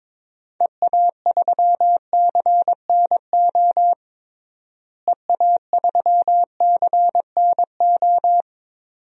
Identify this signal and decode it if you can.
Morse code, heard as a keyed tone, about 22 wpm, 700 Hz. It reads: EA3CNO EA3CNO
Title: DXCC-100 Call Signs - 6 Characters - Repeated 2x - 22wpm